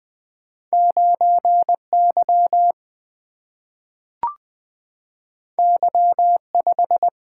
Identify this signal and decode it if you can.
Morse code, heard as a keyed tone, about 20 wpm, 700 Hz. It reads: 9 Y Y 5
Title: Single-Letter Single-Number 20wpm